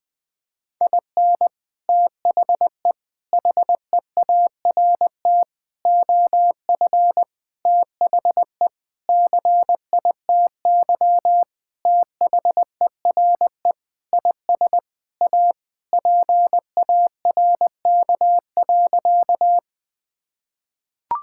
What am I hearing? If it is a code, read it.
Morse code, heard as a keyed tone, about 20 wpm, 700 Hz. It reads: IN THE HEART OF THE CITY THERE IS A PARK.
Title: Sentences with 5-Word Phrases - 20wpm